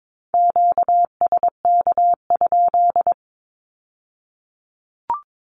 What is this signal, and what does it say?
Morse code, heard as a keyed tone, about 22 wpm, 700 Hz. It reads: QSK?
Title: Q-Codes with Spoken Meaning - 22wpm